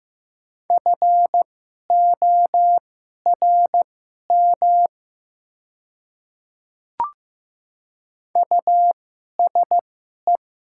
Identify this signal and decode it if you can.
Morse code, heard as a keyed tone, about 15 wpm, spaced as for 11 wpm, 700 Hz. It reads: FORM USE
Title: 2x Character Spacing - Top 200 Words - 15wpm